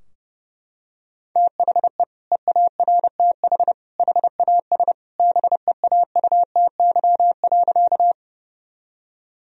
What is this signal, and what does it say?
Morse code, heard as a keyed tone, about 30 wpm, 700 Hz. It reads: THE EARTH HAS BEAUTY.
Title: Sentences from Top 1000 Words - Review Part 2 - 30wpm